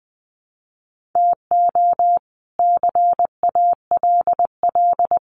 Morse code, heard as a keyed tone, about 20 wpm, 700 Hz. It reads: TO CALL
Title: Sentences from Top 1400 Words 20wpm